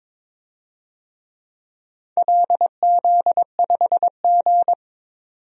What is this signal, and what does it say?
Morse code, heard as a keyed tone, about 22 wpm, 700 Hz. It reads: LZ5G
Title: DXCC-100 Call Signs - 4 Characters - 22wpm